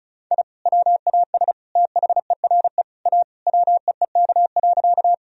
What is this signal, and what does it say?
Morse code, heard as a keyed tone, about 36 wpm, 700 Hz. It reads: I WAS THERE A WEEK.
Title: Sentences from Top 1000 Words - Review Part 2 - 35wpm